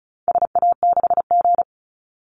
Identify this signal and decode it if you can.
Morse code, heard as a keyed tone, about 34 wpm, 700 Hz. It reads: SA6G
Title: Call Signs - 35wpm